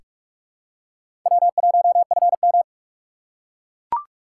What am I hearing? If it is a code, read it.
Morse code, heard as a keyed tone, about 44 wpm, 700 Hz. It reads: W1FM
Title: USA Call Signs - 4 Characters - 4x Character Spacing - 45wpm